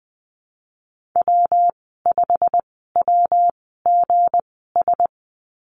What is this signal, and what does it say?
Morse code, heard as a keyed tone, about 19 wpm, spaced as for 14 wpm, 700 Hz. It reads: W5WGS
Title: 167 Lesson 27g - Number 7 - Call Signs 20wpm